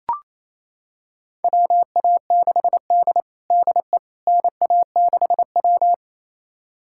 Morse code, heard as a keyed tone, about 28 wpm, 700 Hz. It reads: WA6D DE NA6W
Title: USA Callsign DE Callsign - 4 Characters - 28wpm